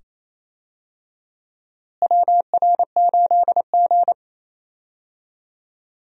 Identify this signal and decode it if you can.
Morse code, heard as a keyed tone, about 28 wpm, 700 Hz. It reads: WR8G
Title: USA Call Signs - 4 Characters - Repeated 2x - 28wpm